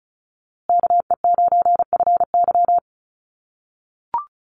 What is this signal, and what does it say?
Morse code, heard as a keyed tone, about 34 wpm, 700 Hz. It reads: KE9FY
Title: USA Call Signs Repeated 2x 35wpm